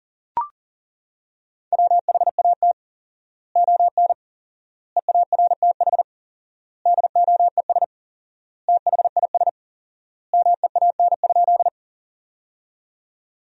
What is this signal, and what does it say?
Morse code, heard as a keyed tone, about 40 wpm, 700 Hz. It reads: WHAT ON EARTH DOES THIS MEAN?
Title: 4x Word-Spacing - Sentences from Top 200 Words - 40wpm